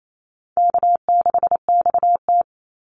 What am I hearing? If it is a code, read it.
Morse code, heard as a keyed tone, about 28 wpm, 700 Hz. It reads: K6XT
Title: Easy Call Signs - 28wpm